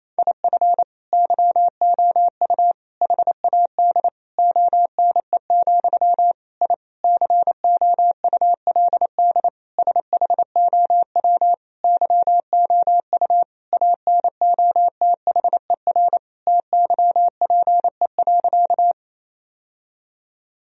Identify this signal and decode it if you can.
Morse code, heard as a keyed tone, about 28 wpm, 700 Hz. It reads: IF YOU HAD ONE, I COULD SHOW YOU ANOTHER TYPE.
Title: Sentences from Top 1000 Words - Review Part 3 - 28wpm